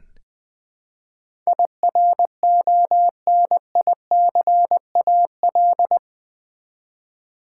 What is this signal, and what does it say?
Morse code, heard as a keyed tone, about 20 wpm, 700 Hz. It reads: IRONICAL
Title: Build A Word - 4 Letters At A Time - 10-Letter Words - 20wpm